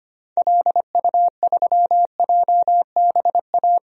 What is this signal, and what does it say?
Morse code, heard as a keyed tone, about 25 wpm, 700 Hz. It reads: LU3JBA
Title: DXCC-100 Call Signs - 6 Characters - Repeated 2x - 25wpm